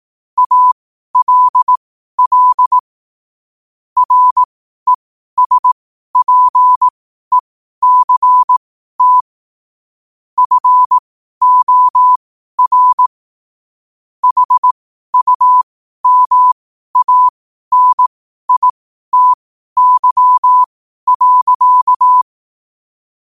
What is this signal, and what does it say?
Morse code, heard as a keyed tone, about 18 wpm, spaced as for 12 wpm, 1000 Hz. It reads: ALL RESPECT FOR HUMANITY.
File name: qotdfiles / 12wpm